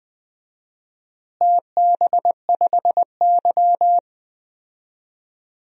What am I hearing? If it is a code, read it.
Morse code, heard as a keyed tone, about 20 wpm, 700 Hz. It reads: TB5Y
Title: DXCC-100 Call Signs - 4 Characters - Repeated 3x - 20wpm